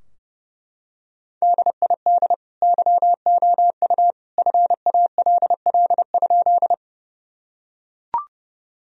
Morse code, heard as a keyed tone, about 32 wpm, 700 Hz. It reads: DID YOU FALL?
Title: Sentences from Top 1000 Words - Review Part 2 - 30wpm